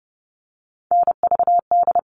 Morse code, heard as a keyed tone, about 30 wpm, 700 Hz. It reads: NVD
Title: All Cut Numbers - 3-Digit 30wpm